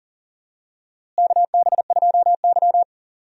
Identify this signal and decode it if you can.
Morse code, heard as a keyed tone, about 40 wpm, 700 Hz. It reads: KB2Y
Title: USA Call Signs - 4 Characters - Repeated 3x - 40wpm